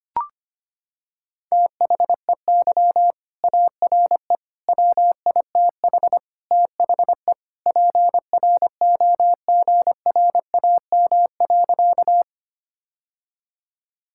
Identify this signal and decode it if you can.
Morse code, heard as a keyed tone, about 25 wpm, 700 Hz. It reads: THEY ARE WITH THE PROGRAM.
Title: Sentences from Top 1000 Words - Review Part 5 - 25wpm